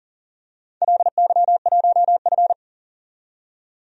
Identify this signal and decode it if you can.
Morse code, heard as a keyed tone, about 44 wpm, 690 Hz. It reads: LY1F
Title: DXCC-100 Call Signs - 4 Characters - 40wpm